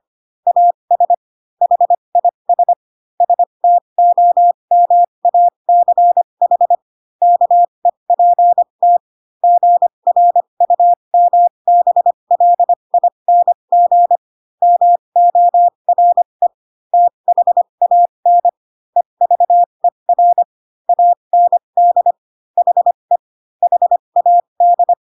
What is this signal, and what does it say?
Morse code, heard as a keyed tone, about 25 wpm, spaced as for 22 wpm, 700 Hz. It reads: AS HIS STOMACH KEPT GRUMBLING MORE THAN EVER AND HE HAD